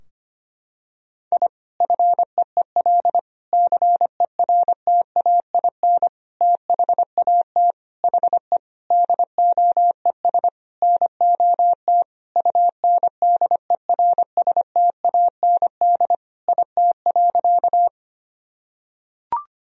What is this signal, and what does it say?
Morse code, heard as a keyed tone, about 25 wpm, 700 Hz. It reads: I FEEL CERTAIN THAT HE DOES NOT UNDERSTAND IT.
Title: Sentences from Top 1000 Words - Review Part 2 - 25wpm